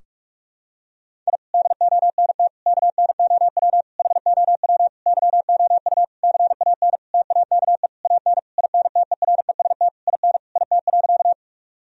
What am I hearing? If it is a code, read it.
Morse code, heard as a keyed tone, about 41 wpm, 700 Hz. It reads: I DONT KNOW HOW YOU CAN TAKE AN INTEREST IN IT.
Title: Sentences from Top 1000 Words - Review Part 2 - 45wpm